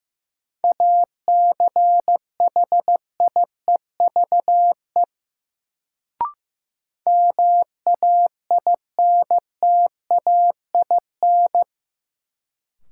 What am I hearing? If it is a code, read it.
Morse code, heard as a keyed tone, about 15 wpm, 700 Hz. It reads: ACHIEVE MAINTAIN
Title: Sentences from Top 1000 Words - Review Part 5 - New Word Drill - 15wpm